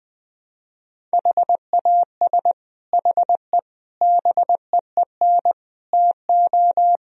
Morse code, heard as a keyed tone, about 21 wpm, 700 Hz. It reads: HAS HE BEEN TO
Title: Sentences from Top 1000 Words - Review Part 1 - 20wpm